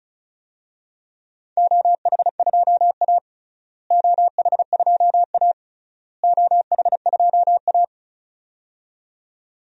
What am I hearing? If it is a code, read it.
Morse code, heard as a keyed tone, about 34 wpm, 700 Hz. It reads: OH2A OH2A OH2A
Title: DXCC-100 Call Signs - 4 Characters - Repeated 3x - 35wpm